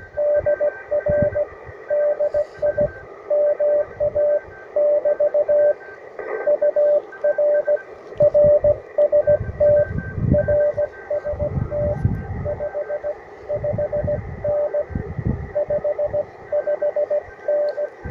Morse code, heard as a keyed tone, about 16 wpm, 590 Hz. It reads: DR DI MA <BT> UR RST RST 55N 55N
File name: attachments / UR4MCK IZ5GVP